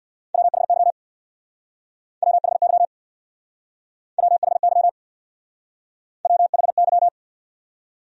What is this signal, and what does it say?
Morse code, heard as a keyed tone, about 60 wpm, 700 Hz. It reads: WHY WHY WHY WHY
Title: Speed-Racing Top 200 Words 50wpm